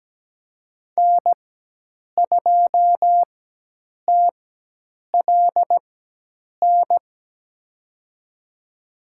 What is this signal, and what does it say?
Morse code, heard as a keyed tone, about 17 wpm, spaced as for 8 wpm, 700 Hz. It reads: N2TLN
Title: USA Call Signs - 5 Characters - 4x Character Spacing - 17wpm